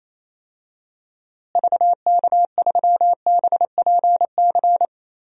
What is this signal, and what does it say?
Morse code, heard as a keyed tone, about 27 wpm, 700 Hz. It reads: VK3BPC
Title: DXCC-100 Call Signs - 6 Characters - Repeated 3x - 28wpm